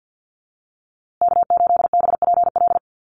Morse code, heard as a keyed tone, about 50 wpm, 700 Hz. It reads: K8BPL
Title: USA Call Signs Repeated 2x 50wpm